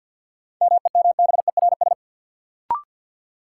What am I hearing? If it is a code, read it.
Morse code, heard as a keyed tone, about 52 wpm, 700 Hz. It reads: MEMBERS
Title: Sets of 3 Words - New Word Drill - 50wpm